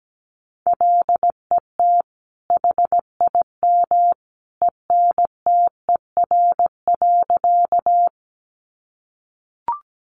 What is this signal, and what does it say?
Morse code, heard as a keyed tone, about 17 wpm, 700 Hz. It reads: LET HIM ENTER.
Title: Sentences from Top 1300 Words 17wpm